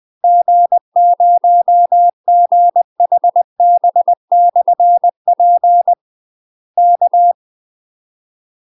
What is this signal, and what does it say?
Morse code, heard as a keyed tone, about 20 wpm, 700 Hz. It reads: G0GHB/P K
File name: UK MASTER QSOs TRAINING / 20WPM / UK MASTER QSOs 0018